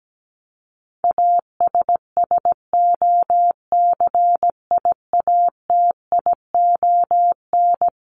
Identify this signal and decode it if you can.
Morse code, heard as a keyed tone, about 17 wpm, 700 Hz. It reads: ASSOCIATION
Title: Top 1001-1250 Words 17wpm